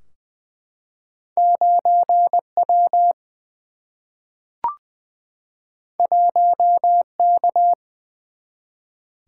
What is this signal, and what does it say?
Morse code, heard as a keyed tone, about 20 wpm, 700 Hz. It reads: 9W 1K
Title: Single-Letter Single-Number 20wpm